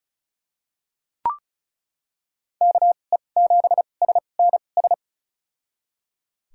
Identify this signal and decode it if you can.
Morse code, heard as a keyed tone, about 36 wpm, spaced as for 25 wpm, 700 Hz. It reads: KE7SNS
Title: USA Call Signs - 6 Characters - 2x Character Spacing - 35wpm